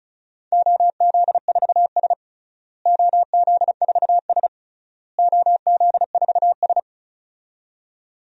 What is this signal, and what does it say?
Morse code, heard as a keyed tone, about 36 wpm, 700 Hz. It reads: OZ4S OZ4S OZ4S
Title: DXCC-100 Call Signs - 4 Characters - Repeated 3x - 35wpm